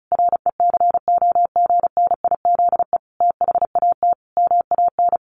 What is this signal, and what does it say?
Morse code, heard as a keyed tone, about 34 wpm, 700 Hz. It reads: RECOGNIZE THAT MAN
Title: Sentences from Top 1400 Words 35wpm